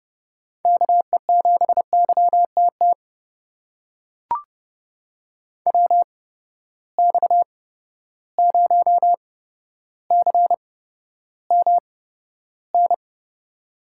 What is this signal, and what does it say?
Morse code, heard as a keyed tone, about 30 wpm, spaced as for 13 wpm, 700 Hz. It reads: KE7YTT WX0CMN
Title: USA Call Signs - 6 Characters - 8x Character Spacing - 30wpm